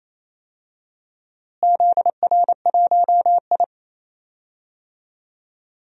Morse code, heard as a keyed tone, about 28 wpm, 700 Hz. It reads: ZR1I
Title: DXCC-100 Call Signs - 4 Characters - Repeated 3x - 28wpm